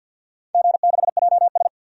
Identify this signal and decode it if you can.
Morse code, heard as a keyed tone, about 51 wpm, 700 Hz. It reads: G6JS
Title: DXCC-100 Call Signs - 4 Characters - 50wpm